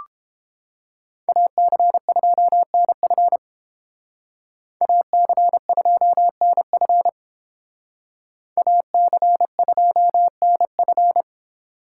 Morse code, heard as a keyed tone, about 30 wpm, 700 Hz. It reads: AC2NF AC2NF AC2NF